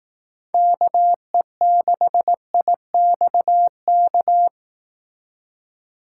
Morse code, heard as a keyed tone, about 18 wpm, 700 Hz. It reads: KE6IXK
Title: Speed Racing - USA 2x3 Call Signs 15wpm